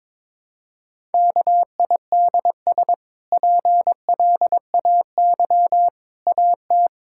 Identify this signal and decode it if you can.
Morse code, heard as a keyed tone, about 22 wpm, 700 Hz. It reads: KIDS PLAY AT
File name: practice / 22wpm / Easy Sentences with Hints - 22wpm